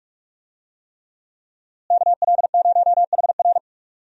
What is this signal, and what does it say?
Morse code, heard as a keyed tone, about 42 wpm, 700 Hz. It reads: KL0HR
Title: USA Call Signs - 5 Characters - Repeated 3x - 45wpm